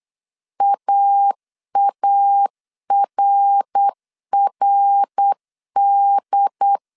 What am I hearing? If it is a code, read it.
Morse code, heard as a keyed tone, about 8 wpm, 800 Hz. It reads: AARRD